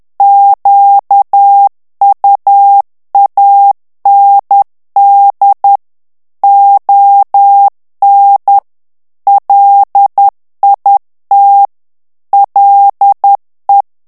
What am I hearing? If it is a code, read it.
Morse code, heard as a keyed tone, about 10 wpm, 800 Hz. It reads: QUAND ON LIT LE
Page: Français - texte en clair 1 - vitesse : 10 mots minute